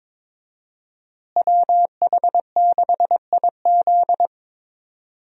Morse code, heard as a keyed tone, about 22 wpm, 700 Hz. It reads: WH6IZ
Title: USA Call Signs - 5 Characters - Repeated 2x - 22wpm